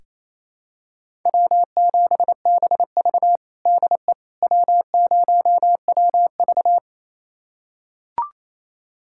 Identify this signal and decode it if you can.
Morse code, heard as a keyed tone, about 28 wpm, 700 Hz. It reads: W7BV DE W0WV
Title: USA Callsign DE Callsign - 4 Characters - 28wpm